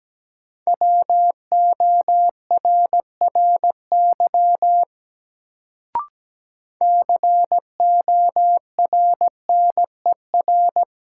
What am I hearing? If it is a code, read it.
Morse code, heard as a keyed tone, about 17 wpm, 700 Hz. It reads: WORRY CORNER
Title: Sets of 5 Words - New Word Drill - 17wpm